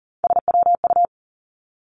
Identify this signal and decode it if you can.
Morse code, heard as a keyed tone, about 39 wpm, 700 Hz. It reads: SWU
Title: Three Challenging Letter-Number Combinations 40wpm